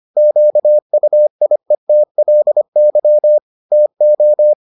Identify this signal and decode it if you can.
Morse code, heard as a keyed tone, about 25 wpm, 600 Hz. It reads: QUIETLY TO